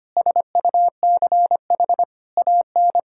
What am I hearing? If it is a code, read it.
Morse code, heard as a keyed tone, about 25 wpm, 700 Hz. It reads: SUCH AN
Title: Sentences from Top 1000 Words - Review Part 5 - 25wpm